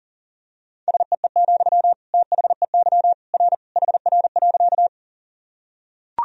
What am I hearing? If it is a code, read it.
Morse code, heard as a keyed tone, about 37 wpm, 700 Hz. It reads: SEE, THEY R HR.
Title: Sentences from Top 200 Words and Abbreviations 40wpm